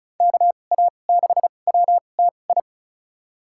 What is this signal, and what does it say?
Morse code, heard as a keyed tone, about 34 wpm, spaced as for 25 wpm, 700 Hz. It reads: KA6WTI
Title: USA Call Signs - 2x Character Spacing - 35wpm